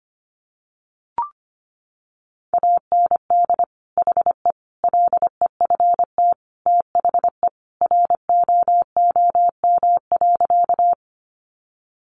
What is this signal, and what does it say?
Morse code, heard as a keyed tone, about 25 wpm, 700 Hz. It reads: AND HE LEFT THE ROOM.
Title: Sentences from Top 300 Words 25wpm